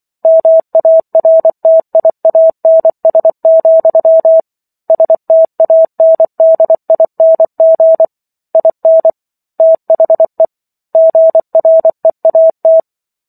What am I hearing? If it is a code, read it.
Morse code, heard as a keyed tone, about 24 wpm, 650 Hz. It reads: MARTIANS, STANDING IN THE GREAT